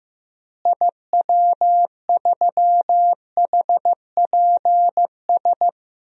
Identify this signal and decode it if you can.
Morse code, heard as a keyed tone, about 15 wpm, 700 Hz. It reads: IW3HPS
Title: DXCC-100 Call Signs - 6 Characters - Repeated 2x - 15wpm